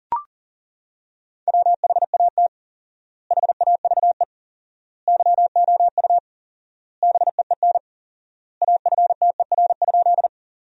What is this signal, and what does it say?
Morse code, heard as a keyed tone, about 37 wpm, 700 Hz. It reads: WHAT HAVE YOU BEEN AFTER?
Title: 4x Word-Spacing - Sentences from Top 200 Words - 40wpm